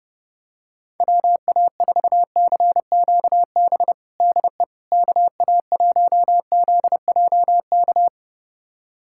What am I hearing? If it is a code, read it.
Morse code, heard as a keyed tone, about 30 wpm, 700 Hz. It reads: WA4CQB DE KA1ZJK
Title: USA Callsign DE Callsign - 6 Characters - 30wpm